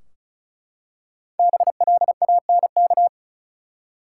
Morse code, heard as a keyed tone, about 36 wpm, 700 Hz. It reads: BLANK
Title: Five Letter Words - 35wpm